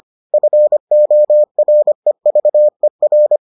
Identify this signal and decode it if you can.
Morse code, heard as a keyed tone, about 25 wpm, 600 Hz. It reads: FOREVER